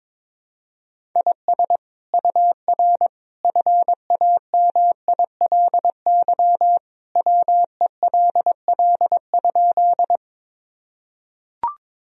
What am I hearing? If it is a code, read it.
Morse code, heard as a keyed tone, about 22 wpm, 700 Hz. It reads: IS UR FAMILY WELL?
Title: Sentences from Top 400 Words and Abbreviations 22wpm